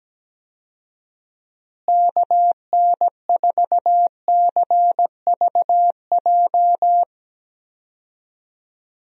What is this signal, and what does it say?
Morse code, heard as a keyed tone, about 17 wpm, 700 Hz. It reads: KN4CVJ
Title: USA Call Signs - 6 Characters - Repeated 2x - 17wpm